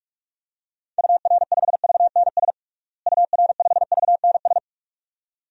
Morse code, heard as a keyed tone, about 46 wpm, 700 Hz. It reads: UR5VNS UR5VNS
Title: DXCC-100 Call Signs - 6 Characters - Repeated 2x - 45wpm